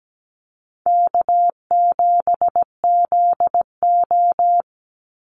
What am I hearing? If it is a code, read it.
Morse code, heard as a keyed tone, about 17 wpm, 700 Hz. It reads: K7ZO
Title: NA SOTA Chaser Callsigns 17wpm